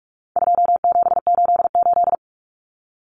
Morse code, heard as a keyed tone, about 44 wpm, 700 Hz. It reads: 2788
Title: Common Cut Numbers - 4-Digit 45wpm